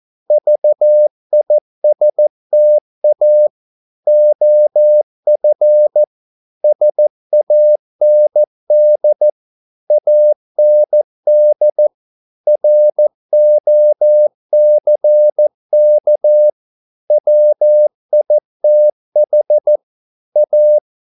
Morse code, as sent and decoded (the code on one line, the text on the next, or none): ...- .. ... - .- / --- ..-. / ... .- -. -.. / .- -. -.. / .-. --- -.-. -.- / .-- .. - .... / .-
VISTA OF SAND AND ROCK WITH A